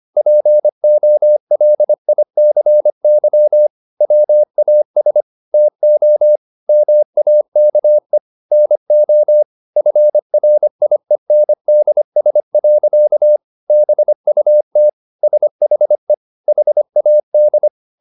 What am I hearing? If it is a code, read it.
Morse code, heard as a keyed tone, about 25 wpm, 600 Hz. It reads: POLICY WAS TO MAKE NO FRIENDS. BUT SHE HAD